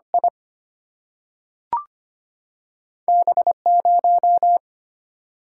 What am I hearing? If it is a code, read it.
Morse code, heard as a keyed tone, about 25 wpm, 700 Hz. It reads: I B 0
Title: Single-Letter Single-Number 25wpm